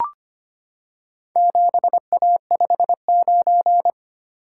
Morse code, heard as a keyed tone, about 25 wpm, 700 Hz. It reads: 7A59
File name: practice / 25wpm / Four Letter-Number Combinations - 25wpm